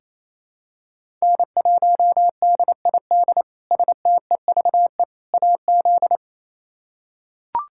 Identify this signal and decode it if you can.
Morse code, heard as a keyed tone, about 28 wpm, 700 Hz. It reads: N1DID STEVE AZ
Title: K1USN SST QSO Practice - 28wpm